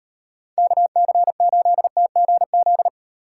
Morse code, heard as a keyed tone, about 38 wpm, 700 Hz. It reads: KC8TGZ